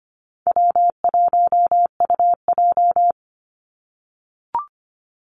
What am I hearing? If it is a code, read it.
Morse code, heard as a keyed tone, about 25 wpm, 700 Hz. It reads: W1UJ
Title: Call Signs - 25wpm